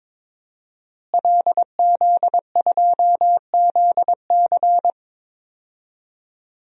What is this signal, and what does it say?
Morse code, heard as a keyed tone, about 22 wpm, 700 Hz. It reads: LZ2ZC
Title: DXCC-100 Call Signs - 5 Characters - Repeated 2x - 22wpm